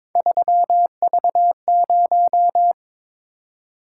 Morse code, heard as a keyed tone, about 22 wpm, 700 Hz. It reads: 3V0
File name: practice / 22wpm / Three Letter-Number Combinations - 22wpm